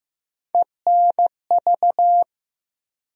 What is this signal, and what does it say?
Morse code, heard as a keyed tone, about 15 wpm, 700 Hz. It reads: ENV
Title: USA Call Sign Postfixes - 3 Letters 15wpm